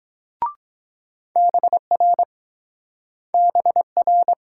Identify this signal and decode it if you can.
Morse code, heard as a keyed tone, about 25 wpm, 700 Hz. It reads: BR BR